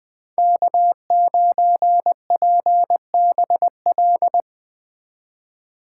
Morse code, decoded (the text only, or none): K9PBL